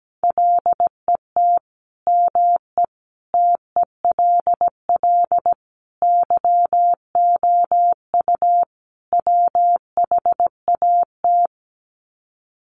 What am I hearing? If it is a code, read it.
Morse code, heard as a keyed tone, about 17 wpm, 700 Hz. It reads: LET ME TELL YOU WHAT
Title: Sets of 5 Words Encore 17wpm